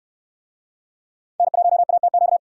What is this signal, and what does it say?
Morse code, heard as a keyed tone, about 68 wpm, 700 Hz. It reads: N1ATQ